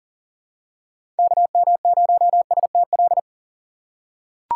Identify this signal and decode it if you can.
Morse code, heard as a keyed tone, about 40 wpm, 700 Hz. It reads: KM0STL
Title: USA Call Signs - 6 Characters - 3x Character Spacing - 40wpm